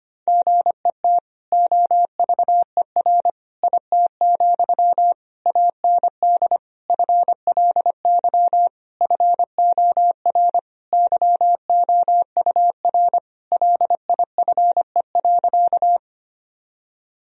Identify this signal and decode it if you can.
Morse code, heard as a keyed tone, about 25 wpm, 700 Hz. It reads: GET OVER IT, AND FLY FOR YOUR LIFE.
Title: Sentences from Top 1000 Words - Review Part 2 - 25wpm